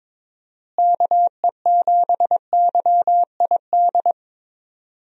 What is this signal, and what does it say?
Morse code, heard as a keyed tone, about 22 wpm, 700 Hz. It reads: KE7YID
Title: USA Call Signs - 6 Characters - Repeated 3x - 22wpm